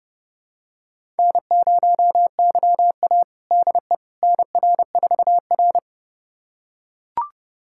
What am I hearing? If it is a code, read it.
Morse code, heard as a keyed tone, about 30 wpm, 700 Hz. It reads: N0YA DE NR4R
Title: USA Callsign DE Callsign - 4 Characters - 30wpm